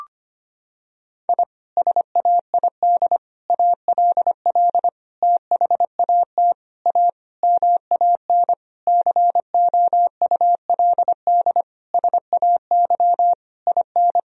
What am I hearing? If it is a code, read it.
Morse code, heard as a keyed tone, about 25 wpm, 700 Hz. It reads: I SAID ALL THAT A MAN COULD SAY IN
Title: Sentences from Top 1000 Words - Review Part 2 - 25wpm